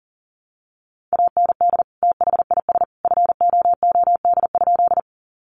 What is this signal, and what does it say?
Morse code, heard as a keyed tone, about 40 wpm, 700 Hz. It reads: AND THIS FOOD?
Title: Sentences from Top 300 Words 40wpm